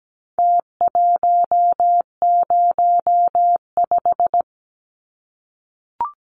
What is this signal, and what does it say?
Morse code, heard as a keyed tone, about 17 wpm, 700 Hz. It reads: T105
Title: Numbers - Contest Style - 4-Digit 17wpm